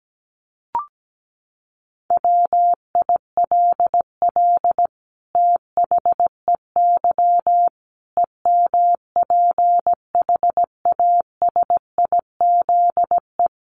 Morse code, heard as a keyed tone, about 17 wpm, 700 Hz. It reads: WILL THEY EMPHASIZE
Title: Sentences from Top 1500 Words 17wpm